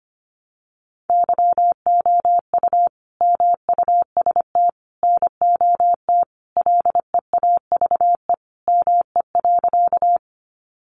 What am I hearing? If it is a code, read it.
Morse code, heard as a keyed tone, about 25 wpm, 700 Hz. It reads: YOU MUST NOT LEAVE ME.
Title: Sentences from Top 600 Words 25wpm